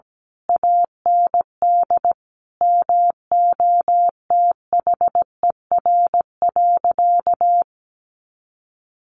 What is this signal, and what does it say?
Morse code, heard as a keyed tone, about 17 wpm, 700 Hz. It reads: AND MOTHER.
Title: Sentences from Top 1100 Words 17wpm